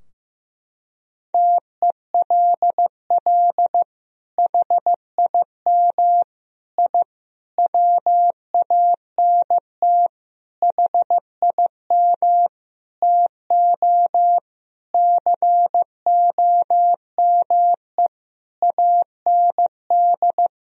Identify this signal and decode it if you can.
Morse code, heard as a keyed tone, about 15 wpm, 700 Hz. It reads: TELL HIM I WANT HIM TO COME AND
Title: Sentences from Top 1000 Words - Review Part 2 - 15wpm